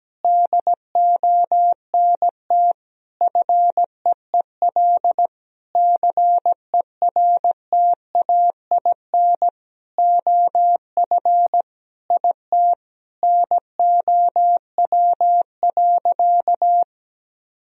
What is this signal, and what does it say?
Morse code, heard as a keyed tone, about 17 wpm, 700 Hz. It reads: DONT FEEL CERTAIN OF IT NOW.
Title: Sentences from Top 1000 Words - Review Part 2 - 17wpm